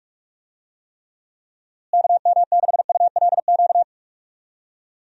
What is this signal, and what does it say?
Morse code, heard as a keyed tone, about 43 wpm, 700 Hz. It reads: KM6ULQ
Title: USA Call Signs - 6 Characters - Repeated 3x - 45wpm